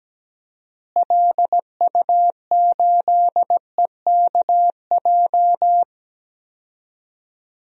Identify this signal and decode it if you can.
Morse code, heard as a keyed tone, about 17 wpm, 700 Hz. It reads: LU8EKJ
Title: DXCC-100 Call Signs - 6 Characters - Repeated 3x - 17wpm